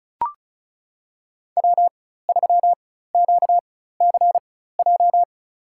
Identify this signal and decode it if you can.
Morse code, heard as a keyed tone, about 34 wpm, spaced as for 16 wpm, 700 Hz. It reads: W3QCJ
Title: USA Call Signs - 5 Characters - 4x Character Spacing - 35wpm